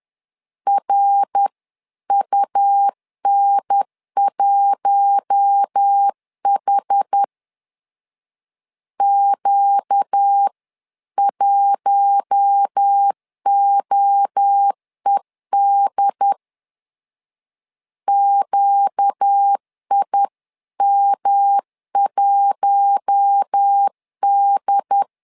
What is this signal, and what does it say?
Morse code, heard as a keyed tone, about 10 wpm, 800 Hz. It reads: RUN1H Q1OED QIM1D
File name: LEZIONI / 4 Q1 10wpm 2448